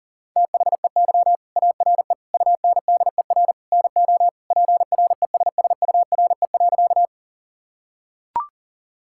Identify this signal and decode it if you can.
Morse code, heard as a keyed tone, about 37 wpm, 700 Hz. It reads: THEY ARE UNDER NO PRESSURE.
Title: Sentences from Top 1000 Words - Review Part 6 - 40wpm